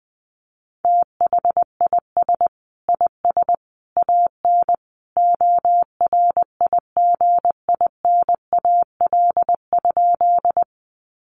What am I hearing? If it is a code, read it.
Morse code, heard as a keyed tone, about 20 wpm, 700 Hz. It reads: THIS IS AN ORIGINAL?
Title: Sentences from Top 900 Words 20wpm